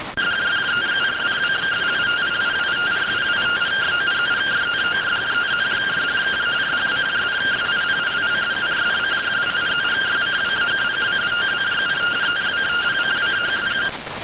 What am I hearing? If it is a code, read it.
Two-tone teleprinter signal, about 170 Hz shift.